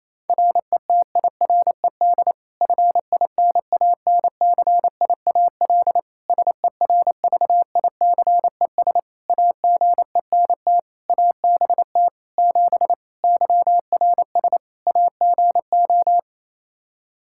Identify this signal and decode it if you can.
Morse code, heard as a keyed tone, about 28 wpm, 700 Hz. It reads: RETIRED FINANCIAL SERVICES AGENT ABT 7 YRS AGO
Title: K1USN SST QSO Practice 28wpm